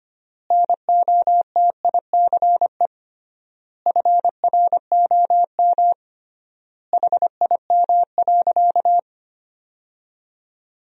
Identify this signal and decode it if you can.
Morse code, heard as a keyed tone, about 25 wpm, 700 Hz. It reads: NOTICE FROM HIM.
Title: 3x Word-Spacing - Sentences from Top 400 Words - 25wpm